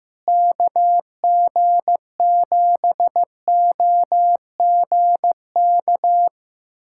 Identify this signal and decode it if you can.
Morse code, heard as a keyed tone, about 15 wpm, 700 Hz. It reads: KG7OGK
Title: USA Callsign DE Callsign 15wpm